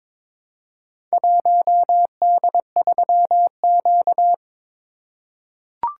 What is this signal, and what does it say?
Morse code, heard as a keyed tone, about 22 wpm, 700 Hz. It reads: 1D3Q
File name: practice / 22wpm / Four Letter-Number Combinations - 22wpm